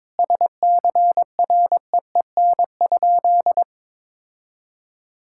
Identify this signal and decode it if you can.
Morse code, heard as a keyed tone, about 22 wpm, 700 Hz. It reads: SCREEN?
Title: Sentences from Top 1000 Words - Review Part 6 - 22wpm